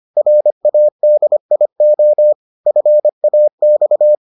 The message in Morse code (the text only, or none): RADIO FAX